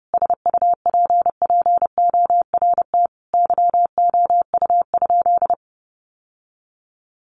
Morse code, heard as a keyed tone, about 30 wpm, 700 Hz. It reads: SUPPORT YOU?
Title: Sentences from Top 800 Words 30wpm